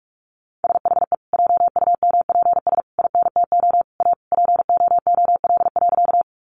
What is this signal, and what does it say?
Morse code, heard as a keyed tone, about 44 wpm, 700 Hz. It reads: SHE JUMPS INTO A POOL.
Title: Easy Phrases 45wpm